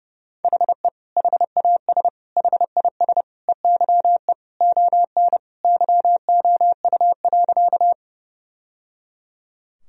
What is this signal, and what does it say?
Morse code, heard as a keyed tone, about 30 wpm, 700 Hz. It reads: HE HAS HIS EYE ON YOU.
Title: Sentences from Top 1000 Words - Review Part 1 - 30wpm